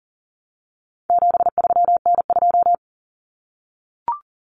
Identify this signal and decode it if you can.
Morse code, heard as a keyed tone, about 40 wpm, 700 Hz. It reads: 73N2
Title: Common Cut Numbers - 4-Digit 40wpm